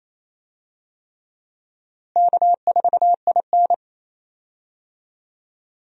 Morse code, heard as a keyed tone, about 28 wpm, 700 Hz. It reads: K4IN
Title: USA Call Signs - 4 Characters - Repeated 3x - 28wpm